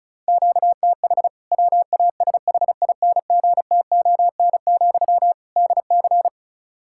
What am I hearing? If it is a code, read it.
Morse code, heard as a keyed tone, about 34 wpm, 700 Hz. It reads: QTH WASHINGTON, DC
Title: K1USN SST QSO Practice 35wpm